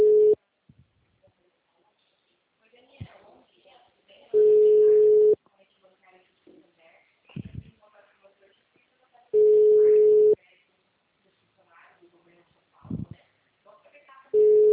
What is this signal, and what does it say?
Ringback tone (Europe, 425 Hz), with speech.